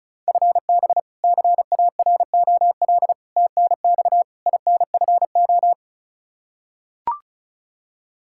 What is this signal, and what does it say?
Morse code, heard as a keyed tone, about 36 wpm, 700 Hz. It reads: FB CAROL TNX INFO
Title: K1USN SST QSO Practice 35wpm